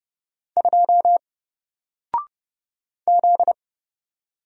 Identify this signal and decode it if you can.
Morse code, heard as a keyed tone, about 29 wpm, 700 Hz. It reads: 2 Z
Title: Single Letter-Number Alt 30wpm